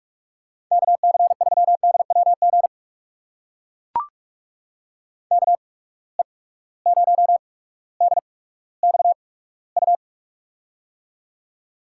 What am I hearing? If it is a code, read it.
Morse code, heard as a keyed tone, about 44 wpm, spaced as for 19 wpm, 700 Hz. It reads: KC3DWG KE0DXU
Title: USA Call Signs - 6 Characters - 8x Character Spacing - 45wpm